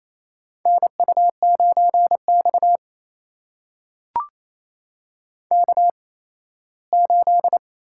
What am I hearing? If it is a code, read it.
Morse code, heard as a keyed tone, about 29 wpm, spaced as for 17 wpm, 700 Hz. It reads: NU9X K8
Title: USA Call Signs - 4 Characters - 8x Character Spacing - 28wpm